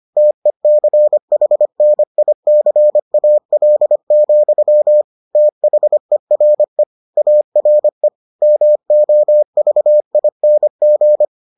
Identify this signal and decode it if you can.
Morse code, heard as a keyed tone, about 25 wpm, 600 Hz. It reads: TECHNICAL, THERE ARE MOVING